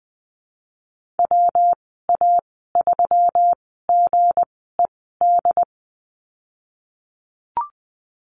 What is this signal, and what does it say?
Morse code, heard as a keyed tone, about 20 wpm, spaced as for 15 wpm, 700 Hz. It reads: WA3GED
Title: 186 Lesson 30g - Letter b - Call Signs 20wpm